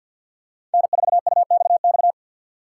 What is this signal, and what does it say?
Morse code, heard as a keyed tone, about 52 wpm, 700 Hz. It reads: N4UXX
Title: USA Call Signs - 5 Characters - 3x Character Spacing - 50wpm